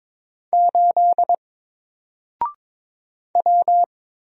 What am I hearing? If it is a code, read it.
Morse code, heard as a keyed tone, about 22 wpm, 700 Hz. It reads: 8 W
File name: Single-Letter-Number-22wpm